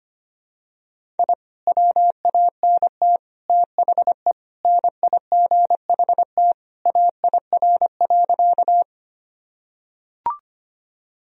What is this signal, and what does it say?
Morse code, heard as a keyed tone, about 25 wpm, 700 Hz. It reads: I WANT THE NIGHT AIR.
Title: Sentences from Top 1000 Words - Review Part 1 - 25wpm